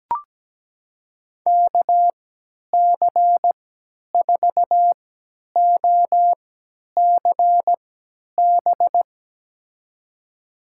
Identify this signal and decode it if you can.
Morse code, heard as a keyed tone, about 17 wpm, spaced as for 10 wpm, 700 Hz. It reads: KC4OCB
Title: USA Call Signs - 6 Characters - 3x Character Spacing - 17wpm